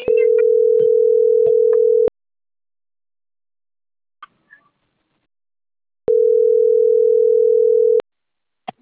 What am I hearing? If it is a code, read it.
Ringback tone (North America), 440 + 480 Hz.